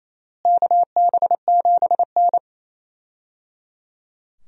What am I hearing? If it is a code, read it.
Morse code, heard as a keyed tone, about 28 wpm, 700 Hz. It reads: KB7N
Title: USA Call Signs - 4 Characters - Repeated 3x - 28wpm